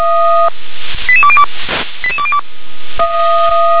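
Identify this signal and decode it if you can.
Five-tone selective call (ZVEI), heard as quick runs of five tones, tones of about 70 ms. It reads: 88202 88202
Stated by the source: ZVEI five-tone selective call received on SDR.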